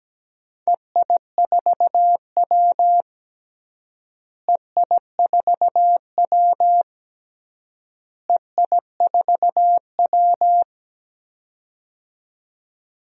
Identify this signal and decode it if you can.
Morse code, heard as a keyed tone, about 17 wpm, 700 Hz. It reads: EI4W EI4W EI4W
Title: DXCC-100 Call Signs - 4 Characters - Repeated 3x - 17wpm